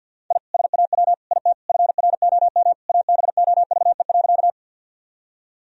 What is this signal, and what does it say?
Morse code, heard as a keyed tone, about 50 wpm, 700 Hz. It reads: I SAW IT FROM ABOVE.
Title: Sentences from Top 1000 Words - Review Part 2 - 50wpm